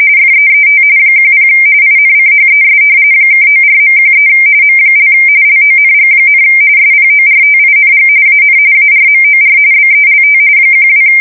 RTTY, about 45.45 baud. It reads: FEET 10 INCHES (94.13 M) LONG, SPORTING AN ARMOR BELT 12
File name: fsk170 / battleships